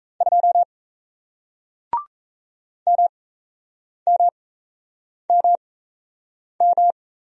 Morse code, heard as a keyed tone, about 36 wpm, spaced as for 8 wpm, 700 Hz. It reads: 2 MMMM